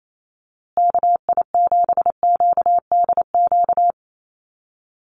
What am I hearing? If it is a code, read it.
Morse code, heard as a keyed tone, about 28 wpm, 700 Hz. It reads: KI7QDQ
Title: USA Call Signs Repeated 2x 28wpm